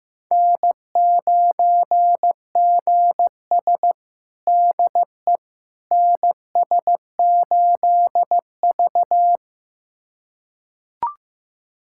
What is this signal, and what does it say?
Morse code, heard as a keyed tone, about 15 wpm, 700 Hz. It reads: N9GS DE NS8V
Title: USA Callsign DE Callsign - 4 Characters - 15wpm